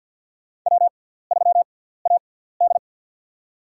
Morse code, heard as a keyed tone, about 48 wpm, spaced as for 17 wpm, 700 Hz. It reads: W3AD
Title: USA Call Signs - 4 Characters - 6x Character Spacing - 50wpm